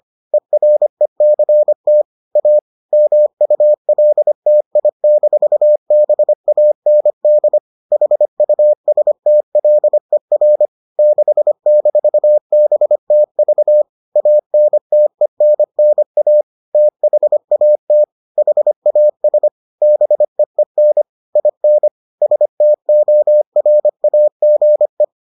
Morse code, heard as a keyed tone, about 25 wpm, 600 Hz. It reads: ERECT A MULTI-BAND HUSTLER 6-BTV ANTENNA THAT HAS BEEN IN STORAGE